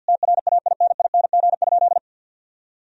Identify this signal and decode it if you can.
Morse code, heard as a keyed tone, about 50 wpm, 700 Hz. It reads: TRAINING?
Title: Sentences from Top 1000 Words - Review Part 4 - 50wpm